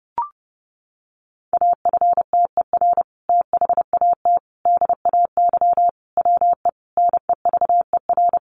Morse code, heard as a keyed tone, about 28 wpm, 700 Hz. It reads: AFTER THAT DAY WE NEVER
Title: Sentences from Top 1100 Words 30wpm